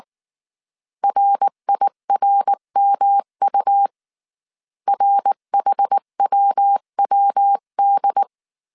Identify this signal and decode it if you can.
Morse code, heard as a keyed tone, about 19 wpm, 800 Hz. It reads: RIRMU RHWWD
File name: CW / DET / 1 3 1500C S B 18wpm 1916